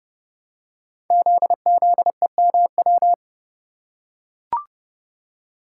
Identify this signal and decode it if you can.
Morse code, heard as a keyed tone, about 30 wpm, 700 Hz. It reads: ZZEMW
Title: Code Groups - 5 Letters - 30wpm